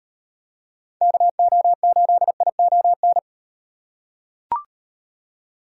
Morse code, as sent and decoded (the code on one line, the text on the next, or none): -.- --- ---.. .. --- -.
KO8ION